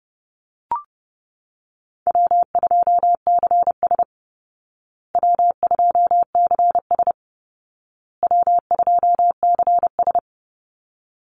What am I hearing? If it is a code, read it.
Morse code, heard as a keyed tone, about 30 wpm, 700 Hz. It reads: W2CS W2CS W2CS
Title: Easy Call Signs Repeated 3x - 30wpm